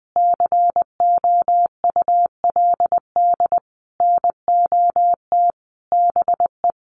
Morse code, heard as a keyed tone, about 20 wpm, 700 Hz. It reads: COULD NOT BE
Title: Sentences from Top 900 Words 20wpm